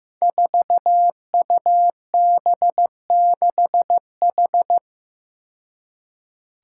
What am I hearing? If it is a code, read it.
Morse code, heard as a keyed tone, about 15 wpm, 700 Hz. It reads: 4UB6H
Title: Five Challenging Letter-Number Combinations 15wpm